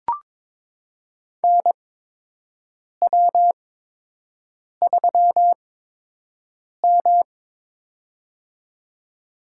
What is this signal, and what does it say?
Morse code, heard as a keyed tone, about 22 wpm, spaced as for 6 wpm, 700 Hz. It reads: NW3M
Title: USA Call Signs - 4 Characters - 8x Character Spacing - 22wpm